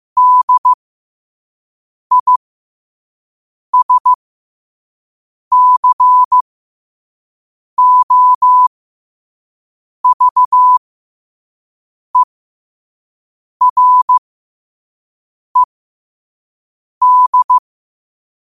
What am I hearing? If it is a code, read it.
Morse code, heard as a keyed tone, about 15 wpm, spaced as for 5 wpm, 1000 Hz. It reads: DISCOVERED
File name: qotdfiles / 5wpm